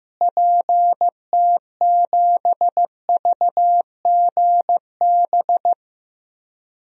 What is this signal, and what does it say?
Morse code, heard as a keyed tone, about 15 wpm, 700 Hz. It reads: PT7VGB
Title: DXCC-100 Call Signs - 6 Characters - Repeated 2x - 15wpm